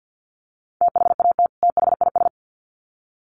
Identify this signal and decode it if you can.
Morse code, heard as a keyed tone, about 50 wpm, 700 Hz. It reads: THAT THIS
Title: Sets of 2 Words 50wpm